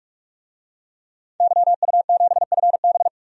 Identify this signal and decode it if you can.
Morse code, heard as a keyed tone, about 41 wpm, 700 Hz. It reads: YU7FB
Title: DXCC-100 Call Signs - 5 Characters - 45wpm